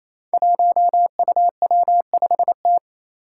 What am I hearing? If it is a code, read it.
Morse code, heard as a keyed tone, about 28 wpm, 700 Hz. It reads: 1UW5T
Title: Five Challenging Letter-Number Combinations 28wpm